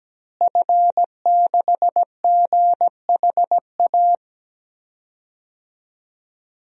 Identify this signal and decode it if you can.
Morse code, heard as a keyed tone, about 17 wpm, 700 Hz. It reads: F6GHA
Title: DXCC-100 Call Signs - 5 Characters - Repeated 2x - 17wpm